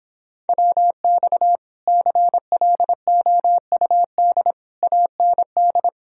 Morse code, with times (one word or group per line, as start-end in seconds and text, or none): 0.5-1.6 WX
1.9-4.5 CLOUD
4.8-5.9 AND